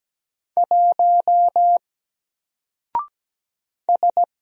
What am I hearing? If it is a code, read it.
Morse code, heard as a keyed tone, about 17 wpm, 700 Hz. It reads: 1 S
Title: Single Character (Letter:Number:Punctuation:Prosign) Alt 17wpm